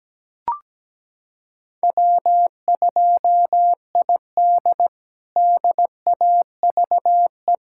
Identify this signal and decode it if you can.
Morse code, heard as a keyed tone, about 17 wpm, 700 Hz. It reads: W2ID DAVE
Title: K1USN SST QSO Practice - 17wpm